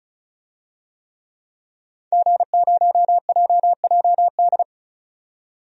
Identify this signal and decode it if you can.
Morse code, heard as a keyed tone, about 34 wpm, 700 Hz. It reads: G0JJD
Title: DXCC-100 Call Signs - 5 Characters - 35wpm